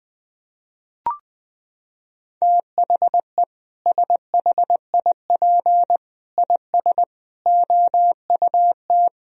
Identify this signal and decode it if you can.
Morse code, heard as a keyed tone, about 21 wpm, 700 Hz. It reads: THE SHIP IS OUT
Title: Sentences from Top 1000 Words - Review Part 4 - 20wpm